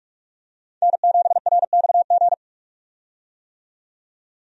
Morse code, heard as a keyed tone, about 46 wpm, 700 Hz. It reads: N7RXG
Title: USA Call Signs - 5 Characters - 45wpm